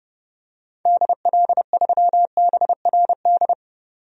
Morse code, heard as a keyed tone, about 30 wpm, 690 Hz. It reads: DL3BRD
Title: DXCC-100 Call Signs - 6 Characters - 30wpm